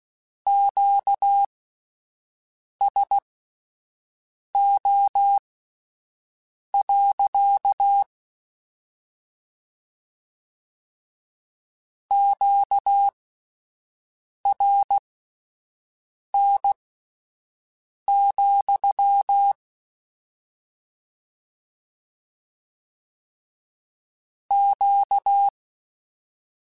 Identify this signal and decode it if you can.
Morse code, heard as a keyed tone, about 16 wpm, spaced as for 5 wpm, 800 Hz. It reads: QSO. QRN, Q
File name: Q Calls / Q Calls-mp3s / Q Calls sppc @10